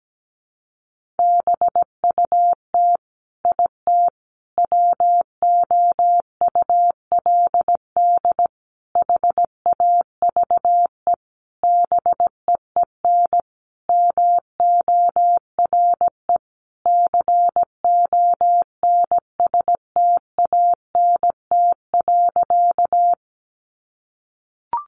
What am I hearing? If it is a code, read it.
Morse code, heard as a keyed tone, about 17 wpm, 700 Hz. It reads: BUT IT WOULD HAVE BEEN MORE CONSTANT.
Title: Sentences from Top 1500 Words 17wpm